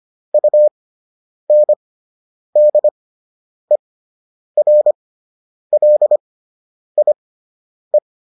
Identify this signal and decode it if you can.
Morse code, heard as a keyed tone, about 25 wpm, spaced as for 9 wpm, 600 Hz. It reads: UNDERLIE